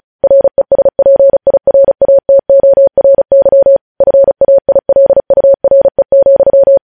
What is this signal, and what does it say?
Morse code, heard as a keyed tone, about 36 wpm, 550 Hz. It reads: RESPIRATORY FAILURE,
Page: News Headlines in Morse Code at 35 WPM